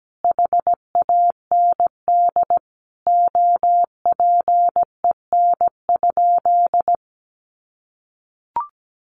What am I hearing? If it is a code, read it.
Morse code, heard as a keyed tone, about 17 wpm, 700 Hz. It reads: HAND OPEN?
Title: Sentences from Top 300 Words 17wpm